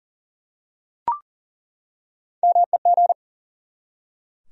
Morse code, heard as a keyed tone, about 40 wpm, 700 Hz. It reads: MEG
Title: USA Common Names - Max 3 Letters - 40wpm